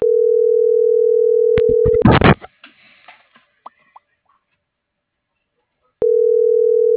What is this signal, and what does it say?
Ringback tone (North America), 440 + 480 Hz.